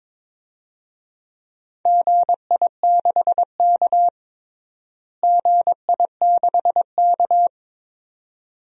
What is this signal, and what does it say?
Morse code, heard as a keyed tone, about 22 wpm, 700 Hz. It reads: GI6K GI6K
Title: DXCC-100 Call Signs - 4 Characters - Repeated 2x - 22wpm